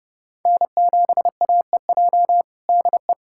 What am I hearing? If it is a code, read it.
Morse code, heard as a keyed tone, about 32 wpm, 700 Hz. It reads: N7AEJ DE
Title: USA Callsign DE Callsign 30wpm